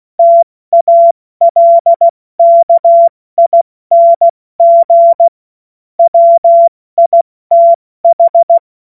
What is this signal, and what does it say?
Morse code, heard as a keyed tone, about 16 wpm, 680 Hz. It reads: TALKING WITH
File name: Morsekurs / Witze / Witz 23 Januar 2024 70 F80